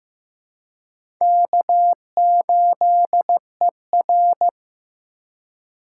Morse code, heard as a keyed tone, about 15 wpm, 700 Hz. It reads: K8ER
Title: USA Call Signs - 4 Characters - 4x Character Spacing - 15wpm